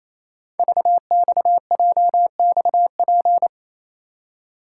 Morse code, heard as a keyed tone, about 28 wpm, 700 Hz. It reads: VXJXP
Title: Code Groups - 5 Letter:Numbers - 28wpm